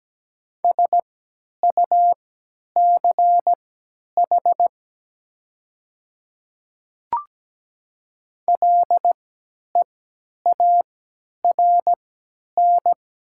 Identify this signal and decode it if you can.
Morse code, heard as a keyed tone, about 17 wpm, spaced as for 10 wpm, 700 Hz. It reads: SUCH LEARN
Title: 3x Character Spacing - Top 300 Words - 17wpm